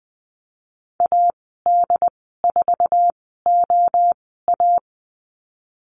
Morse code, heard as a keyed tone, about 20 wpm, spaced as for 15 wpm, 700 Hz. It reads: AD4OA
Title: 088 Lesson 15g - Number 5 - Call Signs 20wpm